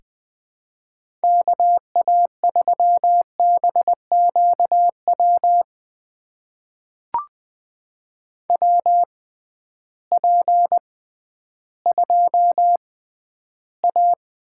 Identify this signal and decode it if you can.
Morse code, heard as a keyed tone, about 20 wpm, spaced as for 12 wpm, 700 Hz. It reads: KA3BQW WP2A
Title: USA Call Signs - 6 Characters - 6x Character Spacing - 20wpm